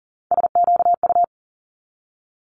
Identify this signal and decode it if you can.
Morse code, heard as a keyed tone, about 38 wpm, 700 Hz. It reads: SQU
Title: Multi-letter Phonemes 40wpm